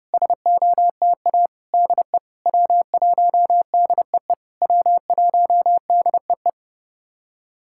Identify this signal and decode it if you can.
Morse code, heard as a keyed tone, about 30 wpm, 700 Hz. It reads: SOTA DE W1DEE W1DEE
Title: SOTA QSO Practice 30wpm